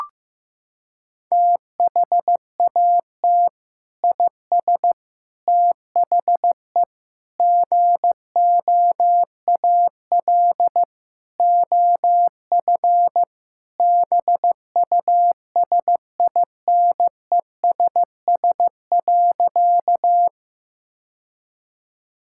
Morse code, heard as a keyed tone, about 16 wpm, 700 Hz. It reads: THAT IS THE GOAL OF BUSINESS.
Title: Sentences from Top 1000 Words - Review Part 5 - 15wpm